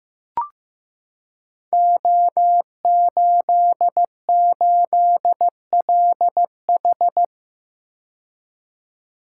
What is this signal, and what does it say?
Morse code, heard as a keyed tone, about 15 wpm, 700 Hz. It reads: O88LH
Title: Code Groups - 5 Letter:Numbers - 15wpm